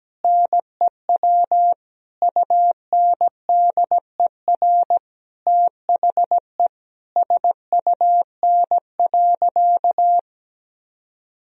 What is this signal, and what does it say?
Morse code, heard as a keyed tone, about 17 wpm, 700 Hz. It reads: NEW UNDER THE SUN.
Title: Sentences from Top 1000 Words - Review Part 1 - 17wpm